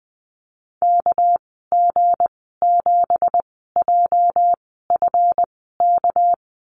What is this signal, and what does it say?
Morse code, heard as a keyed tone, about 20 wpm, spaced as for 14 wpm, 700 Hz. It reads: KG7JFK
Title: 238 Lesson 38g - Letter q - Call Signs 20wpm